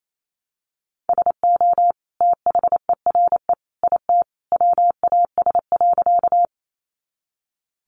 Morse code, heard as a keyed tone, about 28 wpm, 700 Hz. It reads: SO THERE IT WAS.
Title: Sentences from Top 100 Words 28wpm